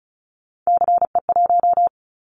Morse code, heard as a keyed tone, about 36 wpm, 700 Hz. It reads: CE1
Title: Callsign Prefixes for Top 100 DXCC - 35wpm